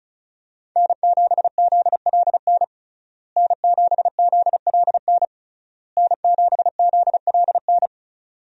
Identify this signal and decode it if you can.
Morse code, heard as a keyed tone, about 36 wpm, 700 Hz. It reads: N7ZLN N7ZLN N7ZLN
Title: USA Call Signs - 5 Characters - Repeated 3x - 35wpm